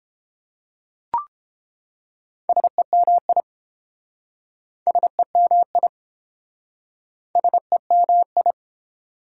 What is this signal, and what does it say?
Morse code, heard as a keyed tone, about 29 wpm, 700 Hz. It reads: SEMI SEMI SEMI